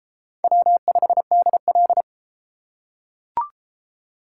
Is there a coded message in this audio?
Yes: Morse code from 0.4 to 2.0 s, heard as a keyed tone, about 28 wpm, 700 Hz. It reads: W5DL